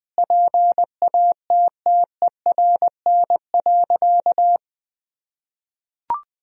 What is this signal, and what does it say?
Morse code, heard as a keyed tone, about 21 wpm, 700 Hz. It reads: PATTERN.
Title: Sentences from Top 1000 Words - Review Part 2 - 20wpm